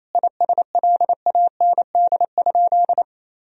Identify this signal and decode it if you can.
Morse code, heard as a keyed tone, about 28 wpm, 700 Hz. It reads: ISLAND?
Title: Sentences from Top 1000 Words - Review Part 3 - 28wpm